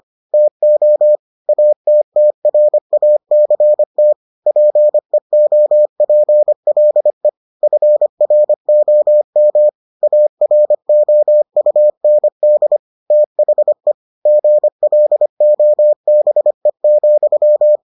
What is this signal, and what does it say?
Morse code, heard as a keyed tone, about 25 wpm, 600 Hz. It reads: TO ATTRACT PEOPLE FROM AROUND THE GLOBE,